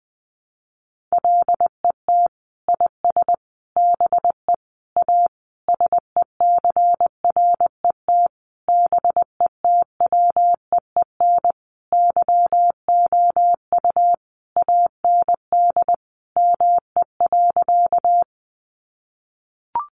Morse code, heard as a keyed tone, about 19 wpm, 700 Hz. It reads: LET IS BE A SECRET BETWEEN YOU AND ME.
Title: Sentences from Top 1200 Words 20wpm